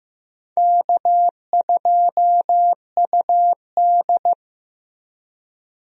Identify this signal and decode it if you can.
Morse code, heard as a keyed tone, about 15 wpm, 700 Hz. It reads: K2UD
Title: USA Call Signs - 4 Characters - Repeated 2x - 15wpm